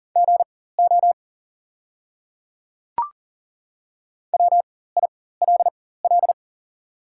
Morse code, heard as a keyed tone, about 40 wpm, spaced as for 19 wpm, 690 Hz. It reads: GO WILL
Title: 4x Character Spacing - Top 100 Words - 40wpm